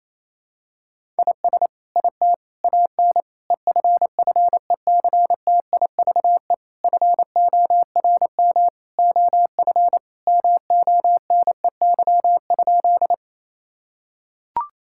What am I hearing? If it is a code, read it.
Morse code, heard as a keyed tone, about 27 wpm, 700 Hz. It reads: IS IT AN EFFECTIVE FORM OF MONEY?
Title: Sentences from Top 1000 Words - Review Part 6 - 28wpm